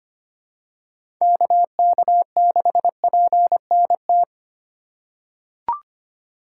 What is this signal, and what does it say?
Morse code, heard as a keyed tone, about 25 wpm, 700 Hz. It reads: KK6PNT
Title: USA Call Signs - 3x Character Spacing - 25wpm